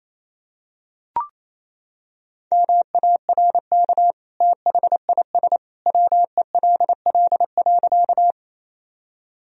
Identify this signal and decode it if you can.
Morse code, heard as a keyed tone, about 28 wpm, 700 Hz. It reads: MARK THIS WELL.
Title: Sentences from Top 1000 Words - Review Part 1 - 28wpm